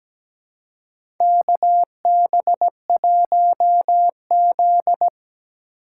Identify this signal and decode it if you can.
Morse code, heard as a keyed tone, about 17 wpm, 700 Hz. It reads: KB1Z
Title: USA Call Signs - 4 Characters - 4x Character Spacing - 17wpm